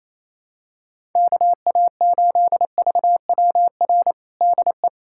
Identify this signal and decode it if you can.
Morse code, heard as a keyed tone, about 28 wpm, 700 Hz. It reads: KA8VWR DE
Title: USA Callsign DE Callsign - 6 Characters - 28wpm